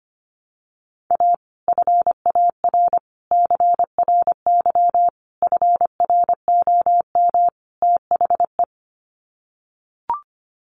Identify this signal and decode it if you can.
Morse code, heard as a keyed tone, about 25 wpm, 700 Hz. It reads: A FAR CRY FROM THE
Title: Sets of 5 Words Encore 25wpm